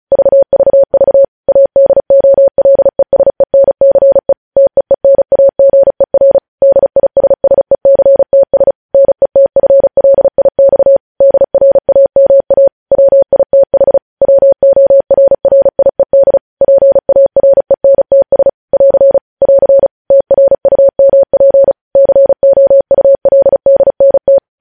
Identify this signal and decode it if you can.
Morse code, heard as a keyed tone, about 36 wpm, 550 Hz. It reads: VVV ADOLESENCE TEENAGER DISSECTS NETFLIX DRAMA WITH WORRIED PARENTS <AR> <AR> TRUMP COULDNT
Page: News headlines in Morse code at 35 wpm